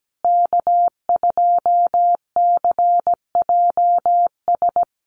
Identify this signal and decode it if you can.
Morse code, heard as a keyed tone, about 17 wpm, 700 Hz. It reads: K2CJS
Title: USA Call Signs Repeated 3x 17wpm